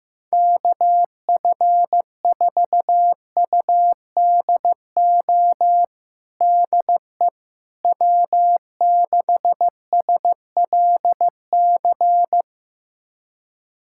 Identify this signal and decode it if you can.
Morse code, heard as a keyed tone, about 15 wpm, 700 Hz. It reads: KF4UDO DE W6SLC
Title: USA Callsign DE Callsign 15wpm